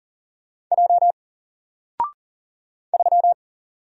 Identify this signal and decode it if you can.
Morse code, heard as a keyed tone, about 40 wpm, 700 Hz. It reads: J 3
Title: Single Letter-Number 40wpm